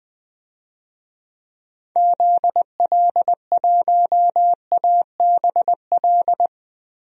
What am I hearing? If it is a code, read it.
Morse code, heard as a keyed tone, about 20 wpm, 700 Hz. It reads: ZL1ABL
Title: DXCC-100 Call Signs - 6 Characters - 20wpm